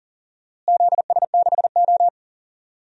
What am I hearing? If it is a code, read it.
Morse code, heard as a keyed tone, about 38 wpm, 700 Hz. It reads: ZS6O
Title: DXCC-100 Call Signs - 4 Characters - Repeated 3x - 40wpm